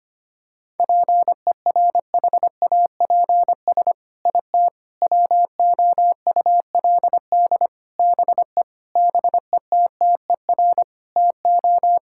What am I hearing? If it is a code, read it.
Morse code, heard as a keyed tone, about 25 wpm, 700 Hz. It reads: PERHAPS IT WOULD BE BETTER TO
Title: Sentences from Top 1600 Words 25wpm